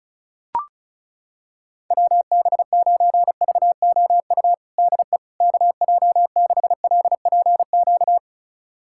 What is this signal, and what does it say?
Morse code, heard as a keyed tone, about 36 wpm, 690 Hz. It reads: WB9VOU DE KJ6LPQ
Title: USA Callsign DE Callsign - 6 Characters - 35wpm